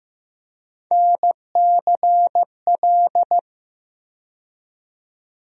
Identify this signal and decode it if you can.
Morse code, heard as a keyed tone, about 15 wpm, 700 Hz. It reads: NCL
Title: USA Call Sign Postfixes - 3 Letters 15wpm